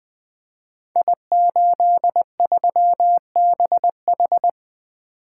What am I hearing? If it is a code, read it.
Morse code, heard as a keyed tone, about 20 wpm, 700 Hz. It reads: I83BH
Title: Code Groups - 5 Letter:Numbers - 20wpm